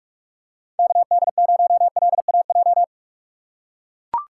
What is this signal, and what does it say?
Morse code, heard as a keyed tone, about 45 wpm, 700 Hz. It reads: KD0LAJ
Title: USA Call Signs - 6 Characters - 6x Character Spacing - 45wpm